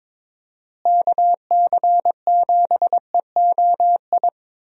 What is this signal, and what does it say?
Morse code, heard as a keyed tone, about 22 wpm, 700 Hz. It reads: KC7EOI
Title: USA Call Signs - 6 Characters - Repeated 2x - 22wpm